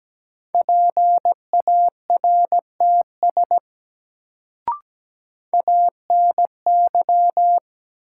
Morse code, heard as a keyed tone, about 17 wpm, 700 Hz. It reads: PARTS ANY
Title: Sets of 3 Words - New Word Drill - 17wpm